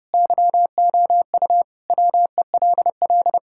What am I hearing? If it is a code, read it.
Morse code, heard as a keyed tone, about 32 wpm, 700 Hz. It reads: YOU WELL
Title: Sentences from Top 1600 Words 30wpm